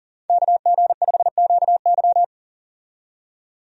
Morse code, heard as a keyed tone, about 40 wpm, 700 Hz. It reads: KG5QY
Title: USA Call Signs - 5 Characters - 40wpm